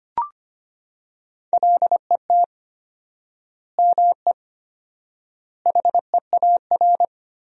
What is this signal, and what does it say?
Morse code, heard as a keyed tone, about 25 wpm, 700 Hz. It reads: LET ME HEAR
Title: 4x Word-Spacing - Sentences from Top 300 Words - 25wpm